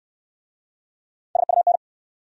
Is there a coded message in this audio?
Yes: Morse code from 1.4 to 1.8 s, heard as a keyed tone, about 69 wpm, 700 Hz. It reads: SUN